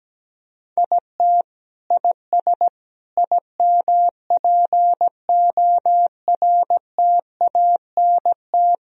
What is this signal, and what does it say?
Morse code, heard as a keyed tone, about 17 wpm, 700 Hz. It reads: IT IS IMPORTANT
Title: Sentences from Top 1000 Words - Review Part 2 - 17wpm